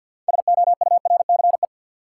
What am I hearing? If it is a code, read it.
Morse code, heard as a keyed tone, about 53 wpm, 700 Hz. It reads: SOURCE
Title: Sentences from Top 800 Words - New Word Drill - 50wpm